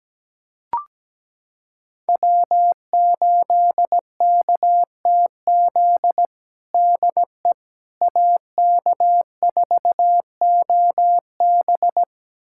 Morse code, heard as a keyed tone, about 17 wpm, 700 Hz. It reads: W8KTZ DE AK4OB
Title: USA Callsign DE Callsign - 5 Characters - 17wpm